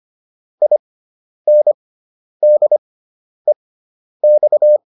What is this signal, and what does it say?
Morse code, heard as a keyed tone, about 25 wpm, spaced as for 10 wpm, 600 Hz. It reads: INDEX